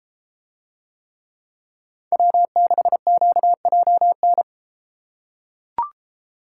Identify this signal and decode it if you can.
Morse code, heard as a keyed tone, about 30 wpm, 700 Hz. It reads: W6QJN